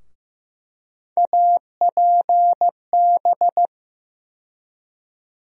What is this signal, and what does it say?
Morse code, heard as a keyed tone, about 15 wpm, 700 Hz. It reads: APB
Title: USA Call Sign Postfixes - 3 Letters 15wpm